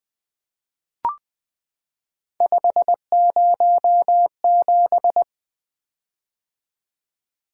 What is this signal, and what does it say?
Morse code, heard as a keyed tone, about 20 wpm, 700 Hz. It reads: 507
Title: Three Letter-Number Combinations - 20wpm